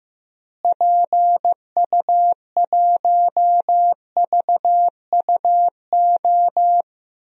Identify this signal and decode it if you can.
Morse code, heard as a keyed tone, about 15 wpm, 700 Hz. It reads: PU1VUO
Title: DXCC-100 Call Signs - 6 Characters - Repeated 3x - 15wpm